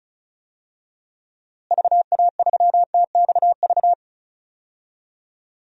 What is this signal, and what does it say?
Morse code, heard as a keyed tone, about 36 wpm, 700 Hz. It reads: VA3TXV
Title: DXCC-100 Call Signs - 6 Characters - Repeated 2x - 35wpm